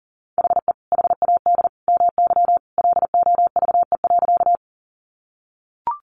Morse code, heard as a keyed tone, about 40 wpm, 700 Hz. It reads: HE HAD MY LOVE.
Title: Sentences from Top 400 Words 40wpm